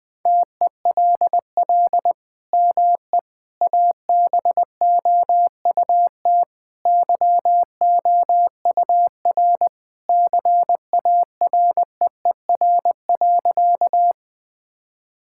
Morse code, heard as a keyed tone, about 20 wpm, 700 Hz. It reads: TELL ME ABOUT YOUR CAREER.
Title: Sentences from Top 1000 Words - Review Part 6 - 20wpm